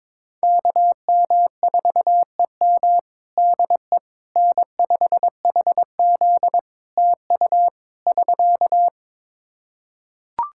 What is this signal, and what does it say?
Morse code, heard as a keyed tone, about 22 wpm, 700 Hz. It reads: KM4EM DE N5HZ TU <SK>
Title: Practice QSO - Exchange 4 - Ending - 22wpm